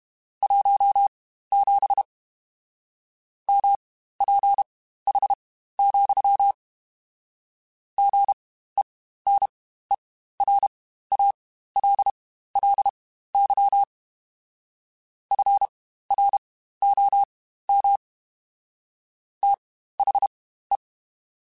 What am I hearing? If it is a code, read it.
Morse code, heard as a keyed tone, about 32 wpm, spaced as for 15 wpm, 800 Hz. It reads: 17 MPH, GENERALLY FROM THE
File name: Spaced Characters